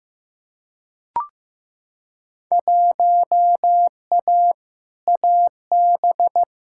Morse code, heard as a keyed tone, about 15 wpm, 700 Hz. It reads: 1A AB
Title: Field Day Exchanges - 15wpm